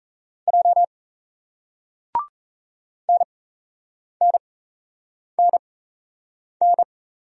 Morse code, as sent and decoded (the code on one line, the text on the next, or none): .--- / -. -. -. -.
J NNNN